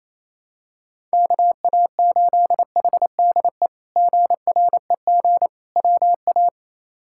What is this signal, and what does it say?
Morse code, heard as a keyed tone, about 28 wpm, 700 Hz. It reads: KA8HDE GREG WA
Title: K1USN SST QSO Practice - 28wpm